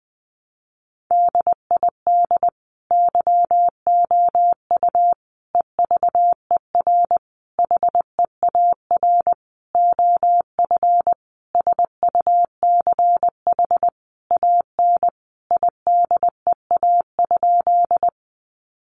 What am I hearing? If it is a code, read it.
Morse code, heard as a keyed tone, about 20 wpm, 700 Hz. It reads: DID YOU EVER HEAR OF SUCH AN IDEA?
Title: Sentences from Top 400 Words 20wpm